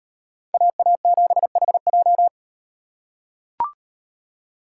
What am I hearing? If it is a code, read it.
Morse code, heard as a keyed tone, about 38 wpm, 700 Hz. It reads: AA7HJ